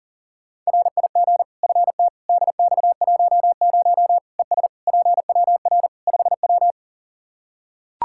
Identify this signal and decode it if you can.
Morse code, heard as a keyed tone, about 40 wpm, 700 Hz. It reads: RIG FT DX10 ES PWR 5W
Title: Practice QSO - Exchange 2 - Info - 40wpm